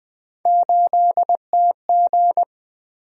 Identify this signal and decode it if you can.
Morse code, heard as a keyed tone, about 20 wpm, 700 Hz. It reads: 8TG
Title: Three Letter-Number Combinations - 20wpm